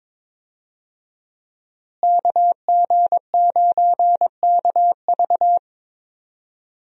Morse code, heard as a keyed tone, about 22 wpm, 700 Hz. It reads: KG9KV
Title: USA Call Signs - 5 Characters - Repeated 2x - 22wpm